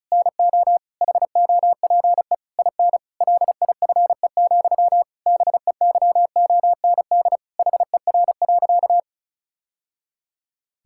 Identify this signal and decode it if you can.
Morse code, heard as a keyed tone, about 34 wpm, 700 Hz. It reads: NO HOPE IN LIFE, BEYOND HER.
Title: Sentences from Top 1000 Words - Review Part 6 - 35wpm